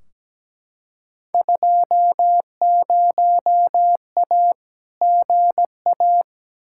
Morse code, heard as a keyed tone, about 17 wpm, 700 Hz. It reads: 20A GA
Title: Field Day Exchanges Encore - 17wpm